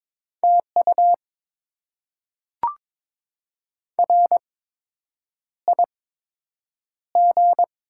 Morse code, heard as a keyed tone, about 22 wpm, spaced as for 8 wpm, 700 Hz. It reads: TU RIG
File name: Top 20 QSO Elements - 8x Character Spacing - 22wpm